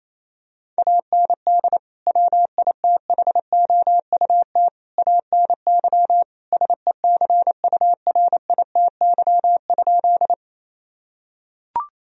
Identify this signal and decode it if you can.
Morse code, heard as a keyed tone, about 28 wpm, 700 Hz. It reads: AND WITHOUT ANY SECURITY?
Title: Sentences from Top 1000 Words - Review Part 4 - 28wpm